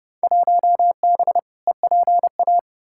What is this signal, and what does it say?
Morse code, heard as a keyed tone, about 30 wpm, 700 Hz. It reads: 1B EPA
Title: Field Day Exchanges Encore - 30wpm